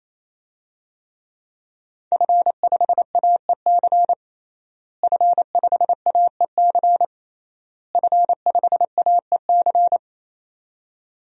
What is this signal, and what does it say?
Morse code, heard as a keyed tone, about 28 wpm, 700 Hz. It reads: F5AEC F5AEC F5AEC
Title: DXCC-100 Call Signs - 5 Characters - Repeated 3x - 28wpm